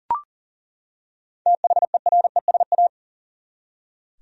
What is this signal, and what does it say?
Morse code, heard as a keyed tone, about 40 wpm, 700 Hz. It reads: THERESA
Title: USA Common Names - 40wpm